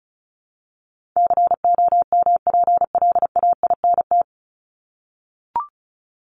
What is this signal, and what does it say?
Morse code, heard as a keyed tone, about 34 wpm, 700 Hz. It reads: COMPLAINT
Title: Top 1251-1500 Words 35wpm